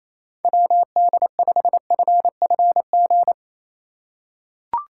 Morse code, heard as a keyed tone, about 28 wpm, 700 Hz. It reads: WD5FFG
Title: USA Call Signs - 6 Characters - 3x Character Spacing - 28wpm